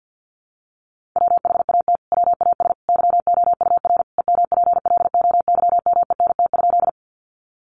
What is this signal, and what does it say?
Morse code, heard as a keyed tone, about 50 wpm, 700 Hz. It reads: WHAT WAS YOUR EMPLOYMENT?
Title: Sentences from Top 1500 Words 50wpm